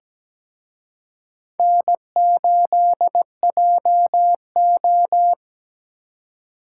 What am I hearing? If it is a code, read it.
Morse code, heard as a keyed tone, about 20 wpm, 700 Hz. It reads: N8JO
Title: Speed Racing - USA 1x2 Call Signs 17wpm